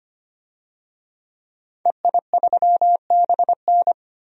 Morse code, heard as a keyed tone, about 25 wpm, 700 Hz. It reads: EI3BN
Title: DXCC-100 Call Signs - 5 Characters - Repeated 3x - 25wpm